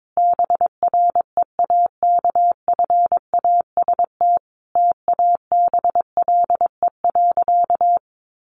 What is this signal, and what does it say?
Morse code, heard as a keyed tone, about 22 wpm, 700 Hz. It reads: BREAKFAST TABLE.
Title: Sentences from Top 1500 Words 22wpm